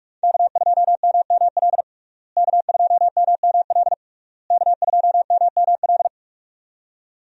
Morse code, heard as a keyed tone, about 41 wpm, 700 Hz. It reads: K2MML K2MML K2MML
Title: USA Call Signs - 5 Characters - Repeated 3x - 45wpm